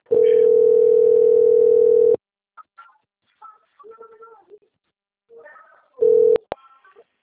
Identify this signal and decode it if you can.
Ringback tone (North America), 440 + 480 Hz.
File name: out- - - - - .21135